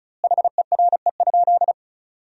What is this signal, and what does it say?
Morse code, heard as a keyed tone, about 36 wpm, 700 Hz. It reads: HERE?
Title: 2x Word-Spacing - Sentences from Top 200 Words - 35wpm